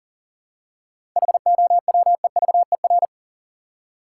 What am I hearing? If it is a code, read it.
Morse code, heard as a keyed tone, about 37 wpm, 700 Hz. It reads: HOWEVER
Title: Seven Letter Words - 40wpm